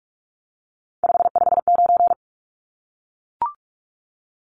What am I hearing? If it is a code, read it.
Morse code, heard as a keyed tone, about 44 wpm, 700 Hz. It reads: 559
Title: Numbers - 3-Digit 45wpm